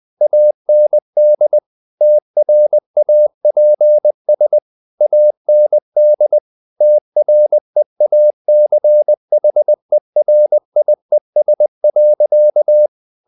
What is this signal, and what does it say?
Morse code, heard as a keyed tone, about 20 wpm, 600 Hz. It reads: AND TRAPS AND TREACHERIES.
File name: the old man and the sea / the old man and the sea part 1 20wpm